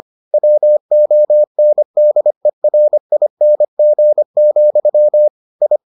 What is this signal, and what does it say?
Morse code, heard as a keyed tone, about 25 wpm, 600 Hz. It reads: WONDERING, I